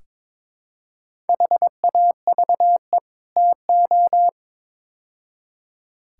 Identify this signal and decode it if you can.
Morse code, heard as a keyed tone, about 22 wpm, 700 Hz. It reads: HAVE TO
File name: practice / 22wpm / Build A Phrase - 5-Word Phrases - 22wpm